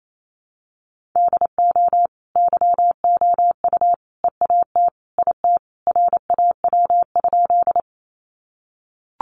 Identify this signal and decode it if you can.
Morse code, heard as a keyed tone, about 28 wpm, 700 Hz. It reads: DO YOU EAT IT RAW?
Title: Sentences from Top 1500 Words 28wpm